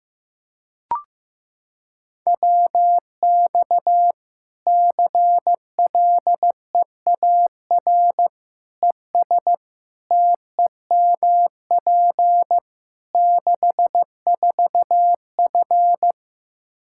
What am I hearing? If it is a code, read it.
Morse code, heard as a keyed tone, about 15 wpm, 700 Hz. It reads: WX CLEAR ES TEMP 64F
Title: K1USN SST QSO Practice 15wpm